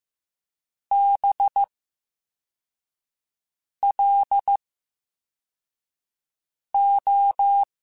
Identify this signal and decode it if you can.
Morse code, heard as a keyed tone, about 15 wpm, 800 Hz. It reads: B L O
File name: Special INGs-mp3s / Special INGs-mp3s / VerySlowToFaster Ing @13